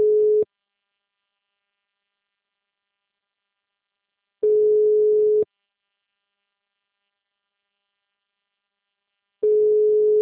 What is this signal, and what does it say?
Ringback tone (Europe, 425 Hz).
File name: out- -2007